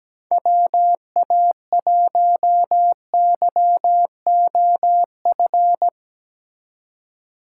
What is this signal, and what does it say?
Morse code, heard as a keyed tone, about 17 wpm, 700 Hz. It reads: WA1YOF
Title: USA Call Signs - 6 Characters - 3x Character Spacing - 17wpm